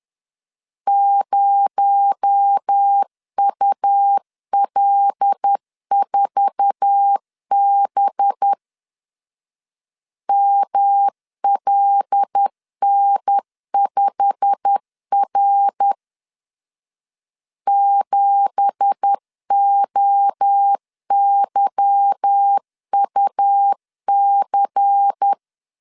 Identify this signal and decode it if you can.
Morse code, heard as a keyed tone, about 10 wpm, 800 Hz. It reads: 0UL4B MLN5R 7OYUC
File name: LEZIONI / 9 2867 10wpm 2733